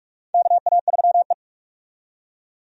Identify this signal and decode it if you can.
Morse code, heard as a keyed tone, about 46 wpm, 700 Hz. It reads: KA3E
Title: USA Call Signs - 4 Characters - 45wpm